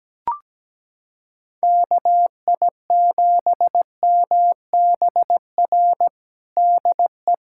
Morse code, heard as a keyed tone, about 17 wpm, 700 Hz. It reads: KI7MBR DE
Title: USA Callsign DE Callsign - 6 Characters - 17wpm